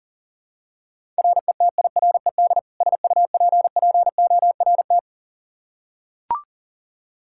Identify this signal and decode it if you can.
Morse code, heard as a keyed tone, about 40 wpm, 700 Hz. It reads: RETIRED SUPPORT
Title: K1USN SST QSO Practice 40wpm